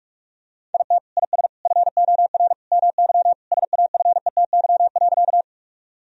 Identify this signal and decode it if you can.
Morse code, heard as a keyed tone, about 41 wpm, 700 Hz. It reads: IT IS FOR MY SAFETY.
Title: Sentences from Top 1000 Words - Review Part 4 - 45wpm